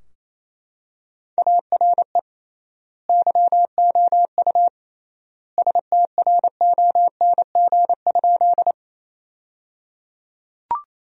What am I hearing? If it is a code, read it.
Morse code, heard as a keyed tone, about 28 wpm, 700 Hz. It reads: ARE YOU STRONG?
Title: 3x Word-Spacing - Sentences from Top 500 Words - 28wpm